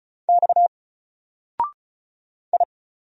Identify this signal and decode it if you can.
Morse code, heard as a keyed tone, about 33 wpm, 700 Hz. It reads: X I
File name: Single-Letter-Number-35wpm